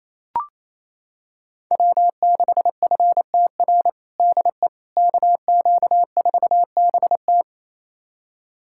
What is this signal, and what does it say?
Morse code, heard as a keyed tone, about 27 wpm, 700 Hz. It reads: W6FTR DE KQ4BT
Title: USA Callsign DE Callsign - 5 Characters - 28wpm